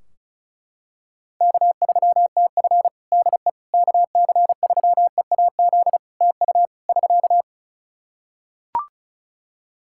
Morse code, heard as a keyed tone, about 36 wpm, 700 Hz. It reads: K3TF DE KC3EAZ TU <SK>
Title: Practice QSO - Exchange 4 - Ending - 35wpm